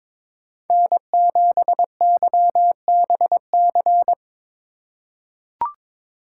Morse code, heard as a keyed tone, about 21 wpm, 700 Hz. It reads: N7YBC
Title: USA Call Signs - 5 Characters - 6x Character Spacing - 22wpm